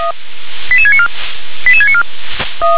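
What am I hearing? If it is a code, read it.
Five-tone selective call (ZVEI), heard as quick runs of five tones, tones of about 70 ms. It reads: 88694 88694
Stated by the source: ZVEI five-tone selective call for 88775, received by SDR.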